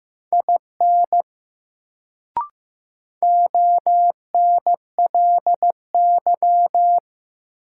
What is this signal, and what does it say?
Morse code, heard as a keyed tone, about 15 wpm, 700 Hz. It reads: IN ONLY
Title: Sets of 2 Words - New Word Drill - 15wpm